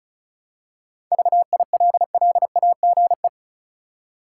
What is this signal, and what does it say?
Morse code, heard as a keyed tone, about 36 wpm, 700 Hz. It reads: VILLAGE
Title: Seven Letter Words - 35wpm